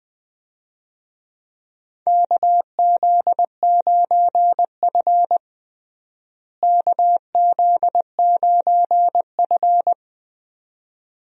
Morse code, heard as a keyed tone, about 20 wpm, 700 Hz. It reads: KZ9F KZ9F
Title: USA Call Signs - 4 Characters - Repeated 2x - 20wpm